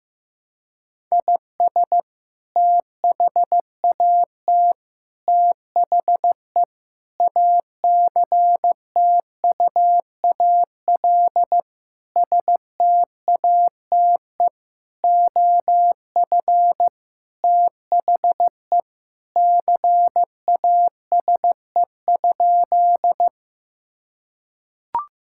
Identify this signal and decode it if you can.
Morse code, heard as a keyed tone, about 15 wpm, 700 Hz. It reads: IS THAT THE ACTUAL STATE OF THE CASE?
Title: Sentences from Top 1000 Words - Review Part 5 - 15wpm